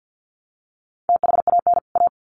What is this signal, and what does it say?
Morse code, heard as a keyed tone, about 50 wpm, 700 Hz. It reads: THAN A
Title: Sets of 2 Words 50wpm